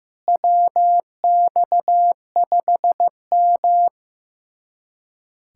Morse code, heard as a keyed tone, about 15 wpm, 700 Hz. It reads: WX5M
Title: USA Callsign DE Callsign - 4 Characters - 15wpm